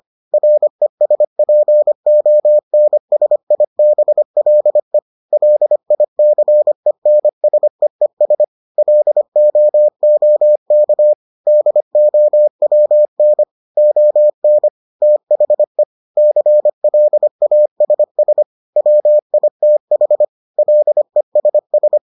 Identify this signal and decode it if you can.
Morse code, heard as a keyed tone, about 25 wpm, 600 Hz. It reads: RESPONSIBLE LICENSEES LOOK DOWN ON THE CLASS WITH LESS